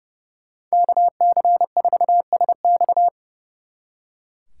Morse code, heard as a keyed tone, about 36 wpm, 700 Hz. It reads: KC4SX
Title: Speed Racing - USA Call Signs 30wpm